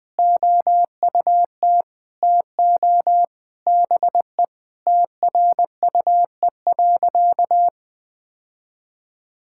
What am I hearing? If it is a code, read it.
Morse code, heard as a keyed tone, about 21 wpm, 700 Hz. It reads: OUT TO BE TRUE.
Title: Sentences with 3-Word Phrases - 20wpm